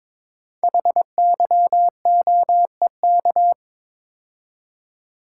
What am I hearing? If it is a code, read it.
Morse code, heard as a keyed tone, about 22 wpm, 700 Hz. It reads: HYOEK
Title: Code Groups - 5 Letters - 22wpm